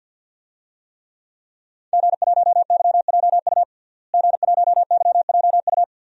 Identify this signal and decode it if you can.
Morse code, heard as a keyed tone, about 51 wpm, 700 Hz. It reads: G1YJU G1YJU
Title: DXCC-100 Call Signs - 5 Characters - Repeated 3x - 50wpm